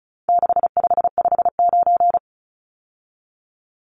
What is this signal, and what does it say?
Morse code, heard as a keyed tone, about 34 wpm, 700 Hz. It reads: 6559
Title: Numbers - 4-Digit 35wpm